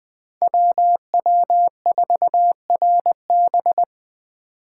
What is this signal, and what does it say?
Morse code, heard as a keyed tone, about 21 wpm, 700 Hz. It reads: WW4RB
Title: Speed Racing - USA 2x2 Call Signs 17wpm